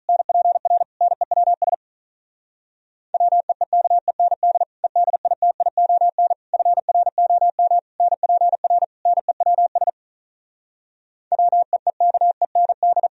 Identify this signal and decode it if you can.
Morse code, heard as a keyed tone, about 46 wpm, 700 Hz. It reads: NPR NEWS WEEKEND EDITION FROM NPR NEWS WEEKEND